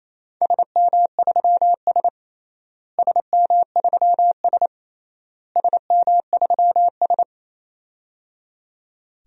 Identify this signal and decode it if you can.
Morse code, heard as a keyed tone, about 28 wpm, 700 Hz. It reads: SM3S SM3S SM3S
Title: DXCC-100 Call Signs - 4 Characters - Repeated 3x - 28wpm